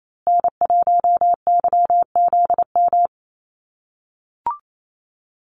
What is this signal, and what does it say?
Morse code, heard as a keyed tone, about 28 wpm, 690 Hz. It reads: N1YZM
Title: USA Call Signs Repeated 3x 28wpm